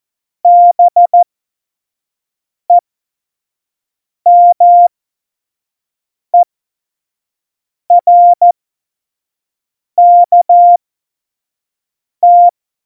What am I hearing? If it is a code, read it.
Morse code, heard as a keyed tone, about 14 wpm, spaced as for 5 wpm, 700 Hz. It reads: BEMERKT
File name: Morsekurs / Witze / Witz 25. Juni 2019, 25 F70